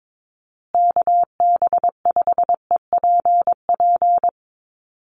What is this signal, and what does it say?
Morse code, heard as a keyed tone, about 22 wpm, 700 Hz. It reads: KB5EPP
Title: USA Call Signs Repeated 2x 22wpm